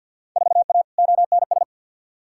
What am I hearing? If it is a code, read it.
Morse code, heard as a keyed tone, about 52 wpm, 700 Hz. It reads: 4A ONS
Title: Field Day Exchanges - 50wpm